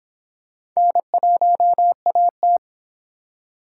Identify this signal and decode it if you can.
Morse code, heard as a keyed tone, about 26 wpm, 700 Hz. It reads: N1AT